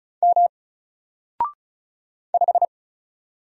Morse code, heard as a keyed tone, about 36 wpm, 700 Hz. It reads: M 5
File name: Single-Letter-Number-35wpm